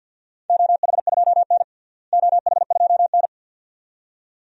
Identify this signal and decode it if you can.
Morse code, heard as a keyed tone, about 51 wpm, 700 Hz. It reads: OH2N OH2N
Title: DXCC-100 Call Signs - 4 Characters - Repeated 2x - 50wpm